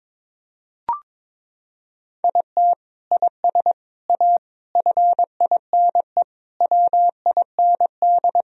Morse code, heard as a keyed tone, about 22 wpm, 700 Hz. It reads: IT IS A FINE WIND
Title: Sentences from Top 1000 Words - Review Part 2 - 22wpm